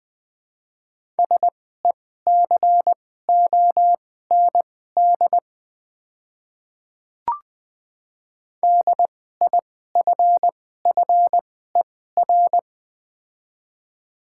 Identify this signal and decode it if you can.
Morse code, heard as a keyed tone, about 21 wpm, spaced as for 15 wpm, 700 Hz. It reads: SECOND DIFFER
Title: 2x Character Spacing - Top 300 Words - 20wpm